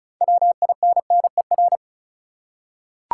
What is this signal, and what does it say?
Morse code, heard as a keyed tone, about 36 wpm, 700 Hz. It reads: WINNER
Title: Six Letter Words - 35wpm